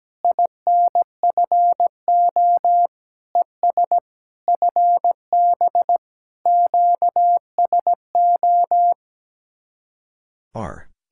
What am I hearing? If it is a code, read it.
Morse code, heard as a keyed tone, about 17 wpm, 700 Hz. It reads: INFO ES FB QSO
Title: Speed-Racing Common QSO Phrases - 17wpm